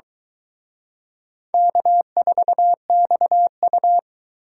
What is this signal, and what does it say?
Morse code, heard as a keyed tone, about 23 wpm, 700 Hz. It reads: K4XU